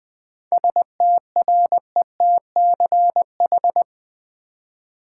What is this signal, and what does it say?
Morse code, heard as a keyed tone, about 20 wpm, 700 Hz. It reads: STRETCH
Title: Seven Letter Words - 20wpm